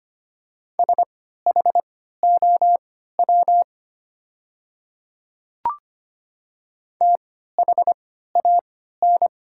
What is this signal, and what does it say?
Morse code, heard as a keyed tone, about 25 wpm, spaced as for 14 wpm, 690 Hz. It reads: SHOW THAN
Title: 3x Character Spacing - Top 200 Words - 25wpm